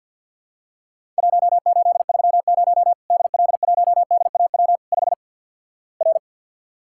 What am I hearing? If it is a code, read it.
Morse code, heard as a keyed tone, about 51 wpm, 700 Hz. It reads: 1830 DL1DAW 5 R
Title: Worked All Europe DX Contest - QTC Exchange - 50wpm